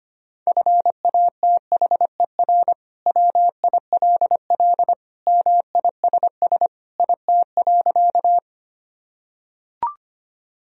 Morse code, heard as a keyed tone, about 25 wpm, 700 Hz. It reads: FATHER WILL MISS IT.
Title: Sentences from Top 1000 Words - Review Part 3 - 25wpm